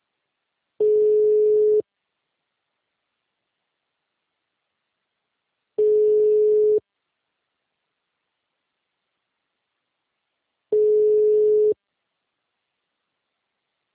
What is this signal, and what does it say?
Ringback tone (Europe, 425 Hz).